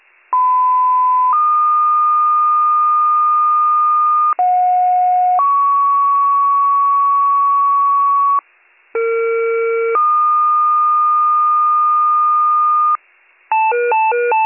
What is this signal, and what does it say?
Two-tone page, heard as a one-second tone, then a three-second tone, about 1010 then 1220 Hz, about 730 then 1120 Hz, about 460 then 1220 Hz. It is heard from 0.3 to 13.0 s.